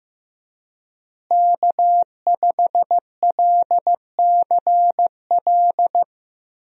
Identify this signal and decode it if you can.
Morse code, heard as a keyed tone, about 15 wpm, 700 Hz. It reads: K5LCL
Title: USA Call Signs - 5 Characters - Repeated 3x - 15wpm